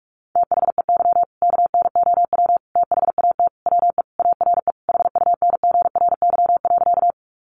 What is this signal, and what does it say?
Morse code, heard as a keyed tone, about 44 wpm, 700 Hz. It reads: THEY KNOW THAT WE ARE HUNGRY.
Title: Sentences from Top 1400 Words 45wpm